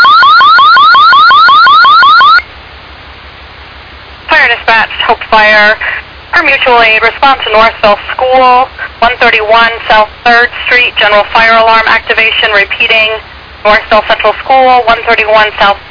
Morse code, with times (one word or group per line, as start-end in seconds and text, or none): none